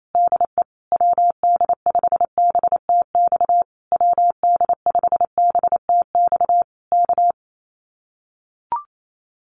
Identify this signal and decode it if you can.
Morse code, heard as a keyed tone, about 28 wpm, 700 Hz. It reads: DE WD5BTX WD5BTX K
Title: Common QSO Phrases Encore 28wpm